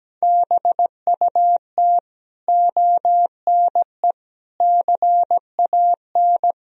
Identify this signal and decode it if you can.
Morse code, heard as a keyed tone, about 17 wpm, 700 Hz. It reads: BUT ONE CAN
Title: Sentences from Top 1000 Words - Review Part 4 - 17wpm